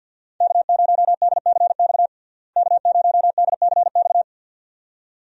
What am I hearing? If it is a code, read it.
Morse code, heard as a keyed tone, about 49 wpm, 700 Hz. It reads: K0DCX K0DCX
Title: USA Call Signs - 5 Characters - Repeated 2x - 50wpm